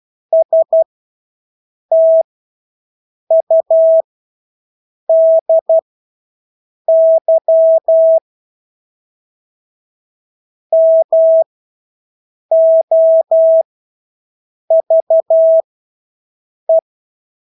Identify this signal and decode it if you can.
Morse code, heard as a keyed tone, about 12 wpm, spaced as for 6 wpm, 650 Hz. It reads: STUDY MOVE
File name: Engelske ord 12-6 wpm